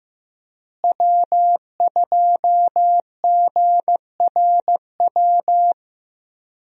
Morse code, heard as a keyed tone, about 15 wpm, 700 Hz. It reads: W2GRW
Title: USA Call Signs - 5 Characters - Repeated 3x - 15wpm